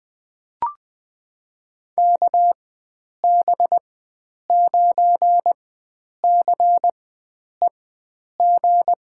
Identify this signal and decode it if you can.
Morse code, heard as a keyed tone, about 20 wpm, spaced as for 9 wpm, 700 Hz. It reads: KB9CEG
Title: USA Call Signs - 6 Characters - 4x Character Spacing - 20wpm